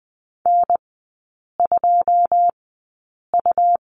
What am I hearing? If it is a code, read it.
Morse code, heard as a keyed tone, about 20 wpm, spaced as for 8 wpm, 700 Hz. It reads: N2U
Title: 198 Lesson 32e - Letter k - Three Characters 20wpm